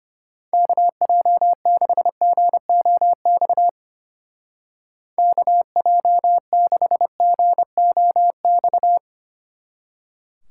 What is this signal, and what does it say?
Morse code, heard as a keyed tone, about 32 wpm, 700 Hz. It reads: KJ6GOX KJ6GOX